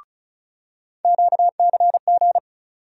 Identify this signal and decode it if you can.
Morse code, heard as a keyed tone, about 34 wpm, 700 Hz. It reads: QCG
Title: USA Call Sign Postfixes - 3 Letters 35wpm